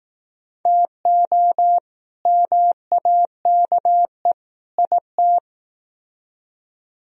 Morse code, heard as a keyed tone, about 19 wpm, 700 Hz. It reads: TO MAKE IT